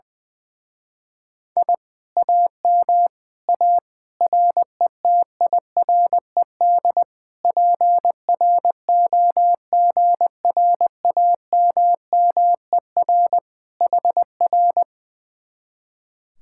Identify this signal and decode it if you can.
Morse code, heard as a keyed tone, about 20 wpm, 700 Hz. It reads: I AM A RETIRED PROGRAMMER HR
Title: Speed-Racing Common QSO Phrases - 20wpm